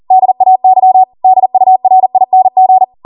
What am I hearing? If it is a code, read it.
Morse code, heard as a keyed tone, about 40 wpm, 750 Hz. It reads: DAY DURING